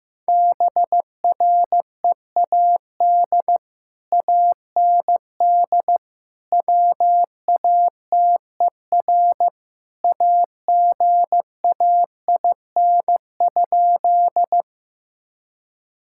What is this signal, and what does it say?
Morse code, heard as a keyed tone, about 15 wpm, 700 Hz. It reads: BREAD AND WATER AGAIN?
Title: Sentences from Top 1600 Words 15wpm